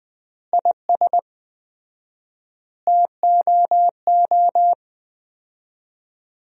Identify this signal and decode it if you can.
Morse code, heard as a keyed tone, about 20 wpm, 700 Hz. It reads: IS TOO
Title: 4x Word-Spacing - Sentences from Top 400 Words - 20wpm